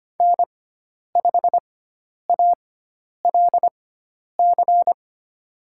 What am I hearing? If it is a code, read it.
Morse code, heard as a keyed tone, about 25 wpm, spaced as for 10 wpm, 700 Hz. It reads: N5ALC
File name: Lesson 16g - Letter c - Call Signs 25wpm